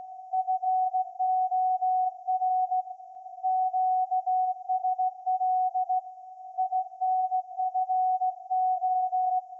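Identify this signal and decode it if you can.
Morse code, heard as a keyed tone, about 16 wpm, 740 Hz. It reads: FOR QSL INFO
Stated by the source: Morse code from SAQ, received at a station in Putignano, Apulia, Italy, on VLF.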